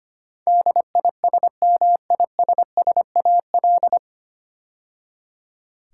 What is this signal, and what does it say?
Morse code, heard as a keyed tone, about 25 wpm, 700 Hz. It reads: DISMISSAL
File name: practice / 25wpm / 9 Letter Words - First 3 Letters Given - 25wpm